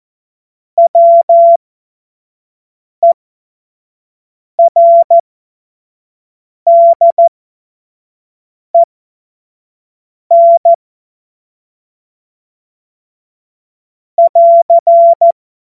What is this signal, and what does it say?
Morse code, heard as a keyed tone, about 14 wpm, spaced as for 5 wpm, 680 Hz. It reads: WERDEN <AR>
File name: Morsekurs / Witze / Witz 17. Januar 2023 25 F70